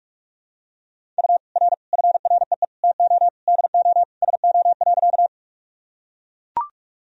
Morse code, heard as a keyed tone, about 41 wpm, 690 Hz. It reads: U R FREE TO DO SO.
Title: Sentences from Top 500 Words and Abbreviations 45wpm